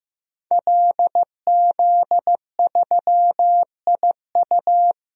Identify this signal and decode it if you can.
Morse code, heard as a keyed tone, about 15 wpm, 700 Hz. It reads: LZ3IU
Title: DXCC-100 Call Signs - 5 Characters - 15wpm